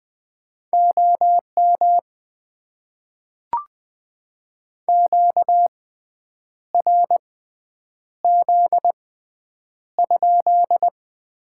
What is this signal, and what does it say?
Morse code, heard as a keyed tone, about 20 wpm, spaced as for 8 wpm, 700 Hz. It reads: OM QRZ?
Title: Top 90 QSO Elements - 6x Character Spacing - 20wpm